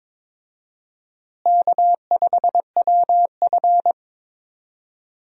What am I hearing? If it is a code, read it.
Morse code, heard as a keyed tone, about 22 wpm, 700 Hz. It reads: K5WF
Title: USA Call Signs - 4 Characters - Repeated 3x - 22wpm